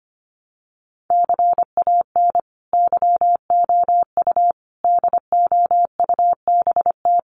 Morse code, heard as a keyed tone, about 25 wpm, 700 Hz. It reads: CAN YOU DOUBT
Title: Sentences from Top 1500 Words 25wpm